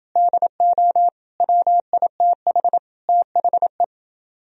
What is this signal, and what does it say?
Morse code, heard as a keyed tone, about 29 wpm, 700 Hz. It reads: DO WITH THE